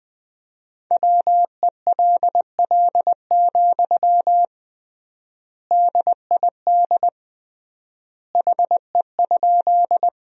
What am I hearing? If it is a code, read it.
Morse code, heard as a keyed tone, about 20 wpm, 700 Hz. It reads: WELL, DID HE?
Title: 3x Word-Spacing - Sentences from Top 200 Words - 20wpm